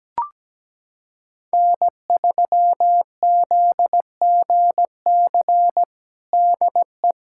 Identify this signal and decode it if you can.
Morse code, heard as a keyed tone, about 17 wpm, 700 Hz. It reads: N3ZGC DE
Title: USA Callsign DE Callsign - 5 Characters - 17wpm